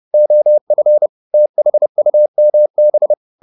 Morse code, heard as a keyed tone, about 30 wpm, 600 Hz. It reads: OF THUMB